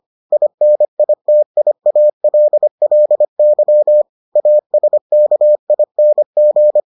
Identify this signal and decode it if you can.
Morse code, heard as a keyed tone, about 25 wpm, 600 Hz. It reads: INITIALLY ASKING